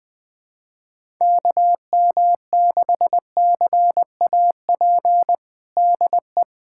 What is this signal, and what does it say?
Morse code, heard as a keyed tone, about 19 wpm, 700 Hz. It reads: KM6CAP DE
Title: USA Callsign DE Callsign 20wpm